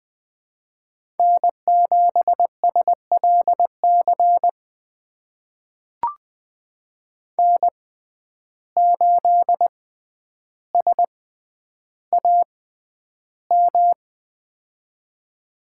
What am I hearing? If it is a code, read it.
Morse code, heard as a keyed tone, about 20 wpm, spaced as for 10 wpm, 700 Hz. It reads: N7SLC N8SAM
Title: USA Call Signs - 5 Characters - 6x Character Spacing - 20wpm